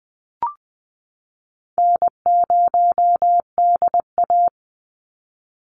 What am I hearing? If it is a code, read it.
Morse code, heard as a keyed tone, about 20 wpm, 700 Hz. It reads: N0DA
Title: NA SOTA Chaser Callsigns 20wpm